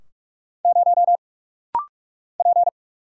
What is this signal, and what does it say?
Morse code, heard as a keyed tone, about 44 wpm, 700 Hz. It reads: 0 P
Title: Single Letter-Number - Rapid-Fire 45wpm